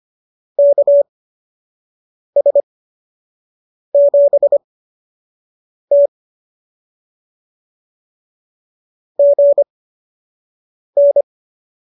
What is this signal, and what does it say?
Morse code, heard as a keyed tone, about 25 wpm, spaced as for 6 wpm, 590 Hz. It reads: KS7T GN